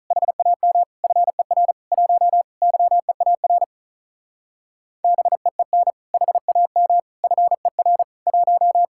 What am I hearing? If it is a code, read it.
Morse code, heard as a keyed tone, about 45 wpm, 700 Hz. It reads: HAM FER 1 YEAR BEEN HAM FER 1